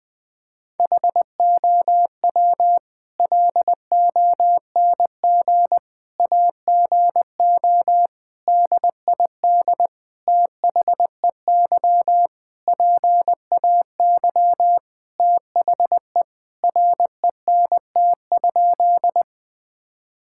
Morse code, heard as a keyed tone, about 20 wpm, 700 Hz. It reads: HOW LONG AGO DID THEY PAY THE RENT?
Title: Sentences from Top 1000 Words - Review Part 5 - 20wpm